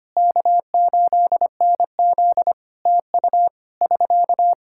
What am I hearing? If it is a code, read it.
Morse code, heard as a keyed tone, about 25 wpm, 700 Hz. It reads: K8NZ TU <SK>
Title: Practice QSO - Exchange 4 - Ending - 25wpm